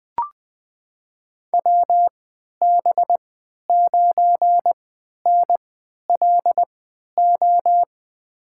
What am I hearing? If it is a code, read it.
Morse code, heard as a keyed tone, about 20 wpm, spaced as for 11 wpm, 700 Hz. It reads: WB9NLO
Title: USA Call Signs - 6 Characters - 3x Character Spacing - 20wpm